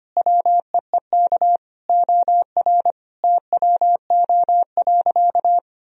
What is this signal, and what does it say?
Morse code, heard as a keyed tone, about 25 wpm, 700 Hz. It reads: WEEK OR TWO.
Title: Sentences from Top 1000 Words - Review Part 2 - 25wpm